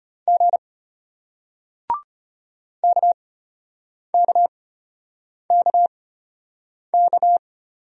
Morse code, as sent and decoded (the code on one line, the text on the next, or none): --. / -.- -.- -.- -.-
G KKKK